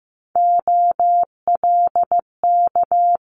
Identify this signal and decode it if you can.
Morse code, heard as a keyed tone, about 15 wpm, 700 Hz. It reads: OLK
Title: Multi-letter Phonemes Quick Fire 15wpm